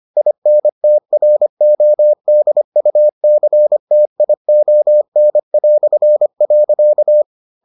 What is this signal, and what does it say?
Morse code, heard as a keyed tone, about 25 wpm, 600 Hz. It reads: INTRODUCTION".